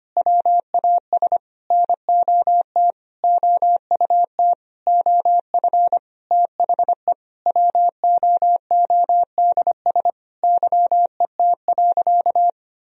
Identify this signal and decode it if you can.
Morse code, heard as a keyed tone, about 25 wpm, 700 Hz. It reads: WAS NOT OUT OF THE WOODS YET.
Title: Sentences from Top 1000 Words - Review Part 2 - 25wpm